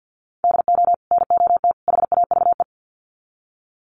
Morse code, heard as a keyed tone, about 50 wpm, 700 Hz. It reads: DO NOT HAVE
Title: Sets of 3 Words 50wpm